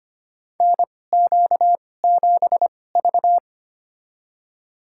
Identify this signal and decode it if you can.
Morse code, heard as a keyed tone, about 25 wpm, spaced as for 18 wpm, 700 Hz. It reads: NQ7V
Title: USA Call Signs - 4 Characters - 2x Character Spacing - 25wpm